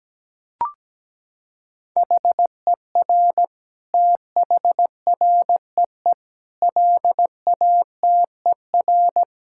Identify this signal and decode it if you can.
Morse code, heard as a keyed tone, about 17 wpm, 700 Hz. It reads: HER THREE LATER
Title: Sentences from Top 1000 Words - Review Part 6 - 17wpm